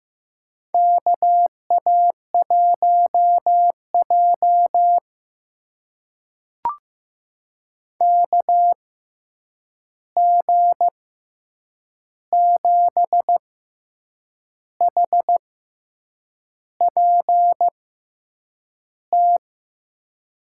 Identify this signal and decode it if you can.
Morse code, heard as a keyed tone, about 15 wpm, spaced as for 7 wpm, 700 Hz. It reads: KA1J KG7HPT
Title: USA Call Signs - 6x Character Spacing - 15wpm